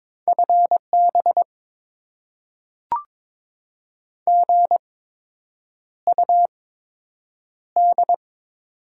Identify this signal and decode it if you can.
Morse code, heard as a keyed tone, about 22 wpm, spaced as for 8 wpm, 700 Hz. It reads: FB GUD
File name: Top 20 QSO Elements - 8x Character Spacing - 22wpm